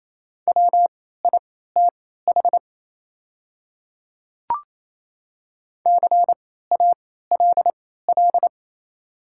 Morse code, heard as a keyed tone, about 28 wpm, spaced as for 16 wpm, 690 Hz. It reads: WITH CALL
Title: 3x Character Spacing - Top 100 Words - 28wpm